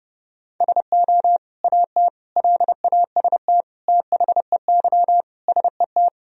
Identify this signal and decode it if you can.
Morse code, heard as a keyed tone, about 32 wpm, 700 Hz. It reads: SO AT LAST THEY SET
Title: Sentences from Top 1000 Words - Review Part 1 - 30wpm